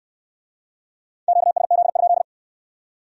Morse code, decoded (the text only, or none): QSZ?